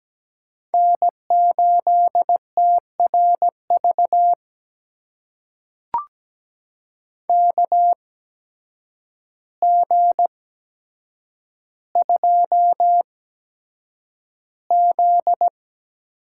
Morse code, heard as a keyed tone, about 17 wpm, spaced as for 8 wpm, 700 Hz. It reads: N8TRV KG2Z
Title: USA Call Signs - 8x Character Spacing - 17wpm